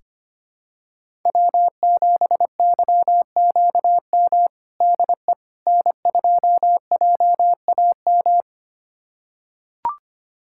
Morse code, heard as a keyed tone, about 25 wpm, 700 Hz. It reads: W7YQM DE N2JAM
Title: USA Callsign DE Callsign - 5 Characters - 25wpm